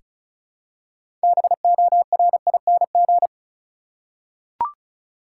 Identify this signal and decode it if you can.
Morse code, heard as a keyed tone, about 36 wpm, 700 Hz. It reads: BORING
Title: Six Letter Words - 35wpm